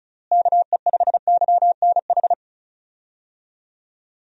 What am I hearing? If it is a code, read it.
Morse code, heard as a keyed tone, about 34 wpm, 700 Hz. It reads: KE5YNH
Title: USA Call Signs - 6 Characters - Repeated 3x - 35wpm